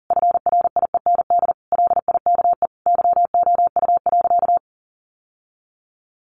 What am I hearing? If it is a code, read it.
Morse code, heard as a keyed tone, about 40 wpm, 700 Hz. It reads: FRIEND LIKE YOU.
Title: Sentences from Top 1100 Words 40wpm